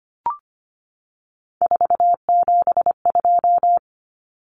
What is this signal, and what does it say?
Morse code, heard as a keyed tone, about 25 wpm, 690 Hz. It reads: 472
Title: Numbers - 3-Digit 25wpm